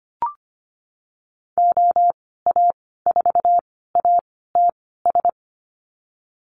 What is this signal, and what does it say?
Morse code, heard as a keyed tone, about 25 wpm, spaced as for 16 wpm, 700 Hz. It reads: OA4ATS
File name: Lesson 15g - Number 5 - Call Signs 25wpm